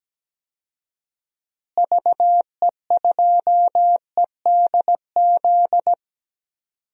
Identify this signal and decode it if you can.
Morse code, heard as a keyed tone, about 17 wpm, 700 Hz. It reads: VE2EDZ
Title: DXCC-100 Call Signs - 6 Characters - Repeated 3x - 17wpm